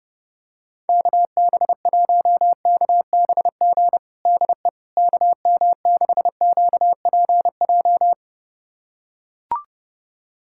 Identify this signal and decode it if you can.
Morse code, heard as a keyed tone, about 30 wpm, 700 Hz. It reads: KB1KBG DE KM6QPJ
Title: USA Callsign DE Callsign 30wpm